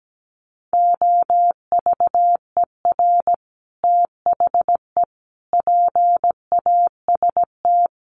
Morse code, heard as a keyed tone, about 17 wpm, 700 Hz. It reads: OVER THE PAST
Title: Sets of 5 Words Encore 17wpm